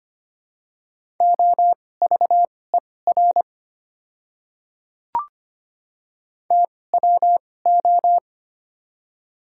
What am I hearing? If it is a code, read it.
Morse code, heard as a keyed tone, about 25 wpm, spaced as for 18 wpm, 700 Hz. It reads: OVER TWO
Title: 2x Character Spacing - Top 300 Words - 25wpm